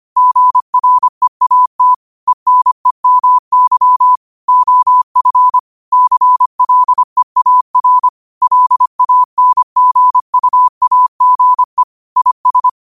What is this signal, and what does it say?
Morse code, heard as a keyed tone, about 25 wpm, 1000 Hz. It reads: GREAT ENEMY OF CLEAR LANGUAGE IS
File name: 25wpm